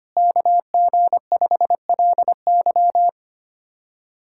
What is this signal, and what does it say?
Morse code, heard as a keyed tone, about 25 wpm, 700 Hz. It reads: KG5LY
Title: USA Callsign DE Callsign - 5 Characters - 25wpm